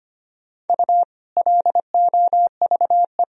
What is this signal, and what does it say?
Morse code, heard as a keyed tone, about 25 wpm, 700 Hz. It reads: U LOVE
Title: Sentences from Top 400 Words and Abbreviations 25wpm